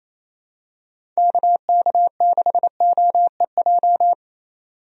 Morse code, heard as a keyed tone, about 28 wpm, 700 Hz. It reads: KK6OEJ
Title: USA Call Signs - 6 Characters - 28wpm